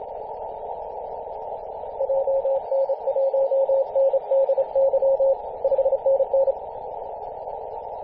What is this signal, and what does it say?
Morse code, heard as a keyed tone, about 29 wpm, 540 Hz, with noise. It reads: JN1NDY 5NN
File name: posts / 2016-arp-15-2115z-vk9l-g7vjr-18-081mhz-cw